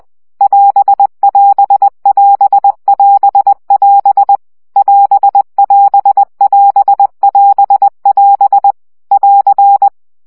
Morse code, heard as a keyed tone, about 20 wpm, 800 Hz. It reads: <AS><AS><AS><AS><AS> <AS><AS><AS><AS><AS> <AR>
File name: CW gyakorlas as 17 baud